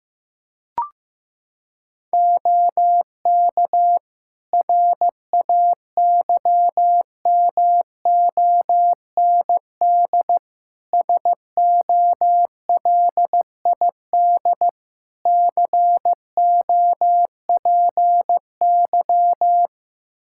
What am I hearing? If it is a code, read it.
Morse code, heard as a keyed tone, about 15 wpm, 700 Hz. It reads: OK RAYMOND SOLID COPY
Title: Practice QSO - Exchange 3 - More Info - 15wpm